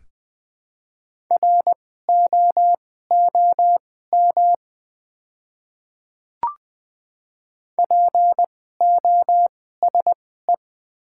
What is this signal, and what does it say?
Morse code, heard as a keyed tone, about 20 wpm, spaced as for 14 wpm, 700 Hz. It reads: ROOM POSE
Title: 2x Character Spacing - Top 500 Words - 20wpm